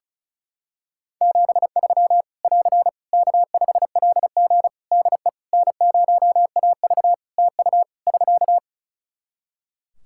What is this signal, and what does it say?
Morse code, heard as a keyed tone, about 36 wpm, 700 Hz. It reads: 73 <AR> K5LG DE N0AV TU <SK>
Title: K1USN SST QSO Practice 35wpm